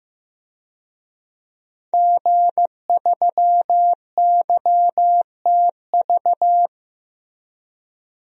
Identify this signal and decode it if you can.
Morse code, heard as a keyed tone, about 16 wpm, 700 Hz. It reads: G3YTV
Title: DXCC-100 Call Signs - 5 Characters - Repeated 3x - 15wpm